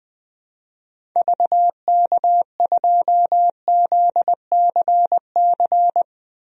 Morse code, heard as a keyed tone, about 20 wpm, 700 Hz. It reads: VK2ZCC
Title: DXCC-100 Call Signs - 6 Characters - Repeated 2x - 20wpm